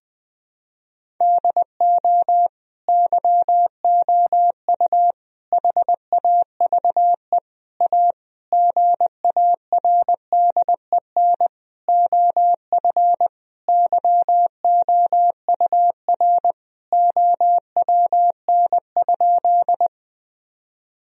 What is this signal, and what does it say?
Morse code, heard as a keyed tone, about 20 wpm, 700 Hz. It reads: DO YOU HAVE A GARDEN OF YOUR OWN?
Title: Sentences from Top 1000 Words - Review Part 4 - 20wpm